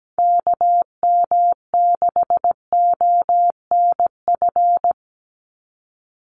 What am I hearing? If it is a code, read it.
Morse code, heard as a keyed tone, about 17 wpm, 700 Hz. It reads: KM6ONF
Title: USA Call Signs 17wpm